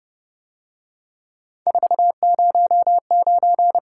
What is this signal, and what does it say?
Morse code, heard as a keyed tone, about 29 wpm, 700 Hz. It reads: 409
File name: Speed-Racing - Three Letter-Number Combinations - 25wpm